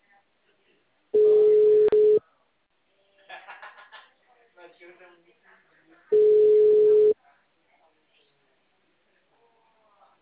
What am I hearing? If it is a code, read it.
Ringback tone (Europe, 425 Hz), with speech.